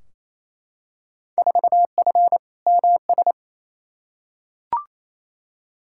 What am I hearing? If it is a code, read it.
Morse code, heard as a keyed tone, about 28 wpm, 700 Hz. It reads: 4F MS
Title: Field Day Exchanges Encore - 28wpm